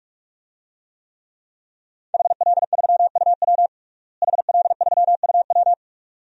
Morse code, heard as a keyed tone, about 46 wpm, 700 Hz. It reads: HL3UW HL3UW
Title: DXCC-100 Call Signs - 5 Characters - Repeated 2x - 45wpm